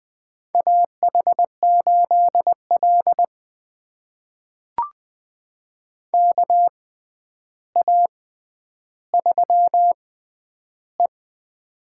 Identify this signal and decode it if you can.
Morse code, heard as a keyed tone, about 21 wpm, spaced as for 10 wpm, 700 Hz. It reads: AH8L KA3E
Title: USA Call Signs - 4 Characters - 6x Character Spacing - 20wpm